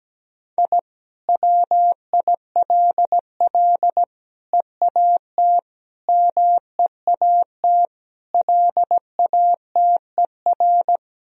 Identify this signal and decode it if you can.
Morse code, heard as a keyed tone, about 17 wpm, 700 Hz. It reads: I WILL EAT MEAT LATER
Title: Sentences from Top 1000 Words - Review Part 3 - 17wpm